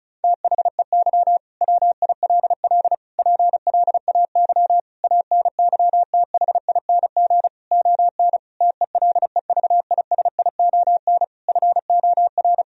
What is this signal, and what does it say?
Morse code, heard as a keyed tone, about 36 wpm, 700 Hz. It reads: THEY WILL PLAY ANYTHING ON TELEVISION FOR
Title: Sentences from Top 1000 Words - Review Part 4 - 35wpm